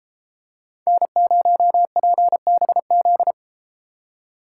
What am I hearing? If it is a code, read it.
Morse code, heard as a keyed tone, about 32 wpm, 700 Hz. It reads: N0PBZ